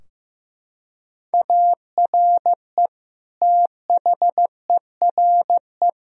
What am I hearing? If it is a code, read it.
Morse code, heard as a keyed tone, about 16 wpm, 700 Hz. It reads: ARE THERE
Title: Sentences from Top 1000 Words - Review Part 4 - 15wpm